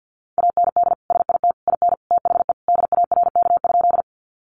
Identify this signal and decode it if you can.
Morse code, heard as a keyed tone, about 50 wpm, 700 Hz. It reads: AND SIT IN THE DARK?
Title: Sentences from Top 500 Words 50wpm